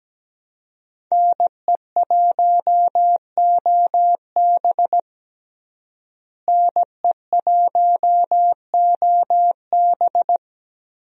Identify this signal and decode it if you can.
Morse code, heard as a keyed tone, about 17 wpm, 700 Hz. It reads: NE1OB NE1OB
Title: USA Call Signs - 5 Characters - Repeated 2x - 17wpm